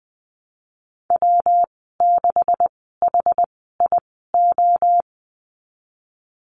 Morse code, heard as a keyed tone, about 20 wpm, spaced as for 15 wpm, 700 Hz. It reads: W6HIO
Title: 147 Lesson 24g - Letter y - Call Signs 20wpm